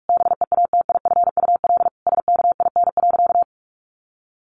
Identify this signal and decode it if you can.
Morse code, heard as a keyed tone, about 44 wpm, 700 Hz. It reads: BEATIFUL SKIN.
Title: Sentences from Top 1300 Words 45wpm